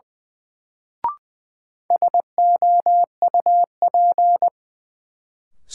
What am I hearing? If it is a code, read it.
Morse code, heard as a keyed tone, about 20 wpm, 700 Hz. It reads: SOUP
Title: Sentences from Top 1600 Words - New Word Drill Alt - 20wpm